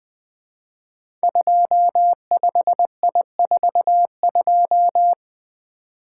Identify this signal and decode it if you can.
Morse code, heard as a keyed tone, about 20 wpm, 700 Hz. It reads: 25I42
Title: Code Groups - 5 Letter:Numbers - 20wpm